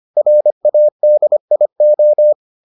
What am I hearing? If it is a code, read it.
Morse code, heard as a keyed tone, about 25 wpm, 600 Hz. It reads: RADIO